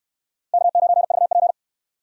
Morse code, heard as a keyed tone, about 68 wpm, 700 Hz. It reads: K9VP